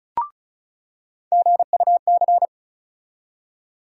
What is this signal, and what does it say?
Morse code, heard as a keyed tone, about 36 wpm, 700 Hz. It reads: GUC
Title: USA Call Sign Postfixes - 3 Letters 35wpm